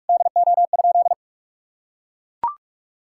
Morse code, heard as a keyed tone, about 43 wpm, 700 Hz. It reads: DO?
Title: Sentences from Top 1600 Words 45wpm